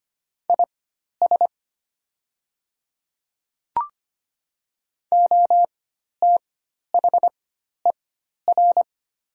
Morse code, heard as a keyed tone, about 25 wpm, spaced as for 12 wpm, 700 Hz. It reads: IS OTHER
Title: 4x Character Spacing - Top 100 Words - 25wpm